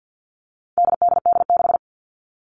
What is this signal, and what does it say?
Morse code, heard as a keyed tone, about 50 wpm, 700 Hz. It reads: DDD6
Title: All Cut Numbers - 4-Digit 50wpm